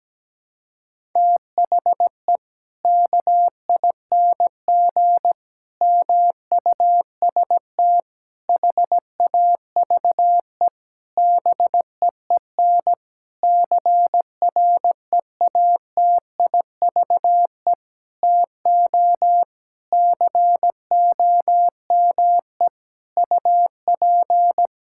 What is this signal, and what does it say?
Morse code, heard as a keyed tone, about 17 wpm, 700 Hz. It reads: THE KING MUST HAVE BEEN CREATIVE TO COME UP
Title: Sentences from Top 1000 Words - Review Part 5 - 17wpm